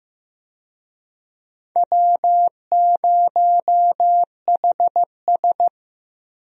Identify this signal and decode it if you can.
Morse code, heard as a keyed tone, about 15 wpm, 700 Hz. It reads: W0HS
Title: USA Call Signs - 4 Characters - Repeated 2x - 15wpm